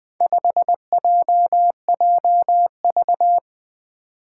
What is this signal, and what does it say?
Morse code, heard as a keyed tone, about 20 wpm, 700 Hz. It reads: 5JJV
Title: Four Challenging Letter-Number Combinations 20wpm